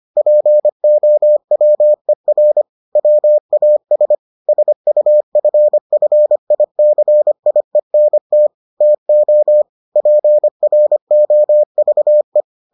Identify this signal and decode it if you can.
Morse code, heard as a keyed tone, about 25 wpm, 600 Hz. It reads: POWER WAS SUFFICIENT TO PROVE